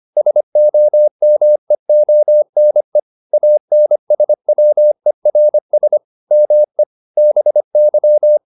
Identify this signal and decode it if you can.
Morse code, heard as a keyed tone, about 25 wpm, 600 Hz. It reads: SOMEONE ANSWERS ME BY